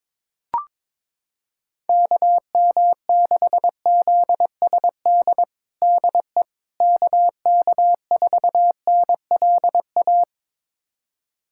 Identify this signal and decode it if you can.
Morse code, heard as a keyed tone, about 21 wpm, 700 Hz. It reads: KM6ZSD DE KK4NLA
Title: USA Callsign DE Callsign - 6 Characters - 22wpm